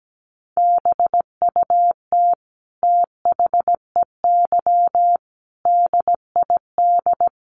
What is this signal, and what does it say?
Morse code, heard as a keyed tone, about 17 wpm, 700 Hz. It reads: BUT THEY DID
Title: Sentences from Top 1400 Words 17wpm